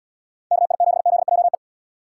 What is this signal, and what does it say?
Morse code, heard as a keyed tone, about 75 wpm, 700 Hz. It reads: KI7ZJE